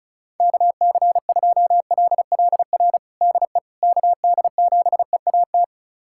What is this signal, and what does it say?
Morse code, heard as a keyed tone, about 34 wpm, 690 Hz. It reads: KC2LLR DE KD7EAT
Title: USA Callsign DE Callsign - 6 Characters - 35wpm